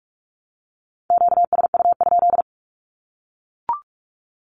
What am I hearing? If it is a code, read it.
Morse code, heard as a keyed tone, about 44 wpm, 700 Hz. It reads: QSU?
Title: Q-Codes Repeated 3x 45wpm